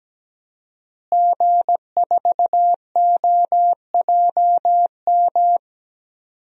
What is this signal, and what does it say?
Morse code, heard as a keyed tone, about 17 wpm, 700 Hz. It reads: G4OJM
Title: DXCC-100 Call Signs - 5 Characters - Repeated 3x - 17wpm